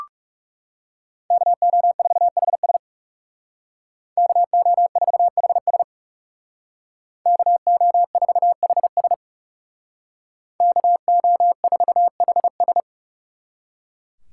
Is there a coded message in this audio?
Yes: Morse code from 1.3 to 12.8 s, heard as a keyed tone, about 39 wpm, 700 Hz. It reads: KO4HS KO4HS KO4HS KO4HS